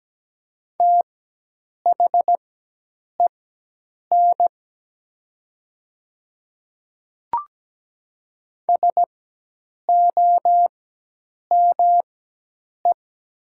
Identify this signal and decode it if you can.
Morse code, heard as a keyed tone, about 17 wpm, spaced as for 8 wpm, 700 Hz. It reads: THEN SOME
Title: 4x Character Spacing - Top 100 Words - 17wpm